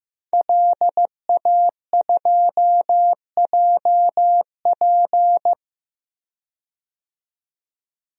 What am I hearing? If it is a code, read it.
Morse code, heard as a keyed tone, about 15 wpm, 700 Hz. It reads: LA2JP
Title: DXCC-100 Call Signs - 5 Characters - 15wpm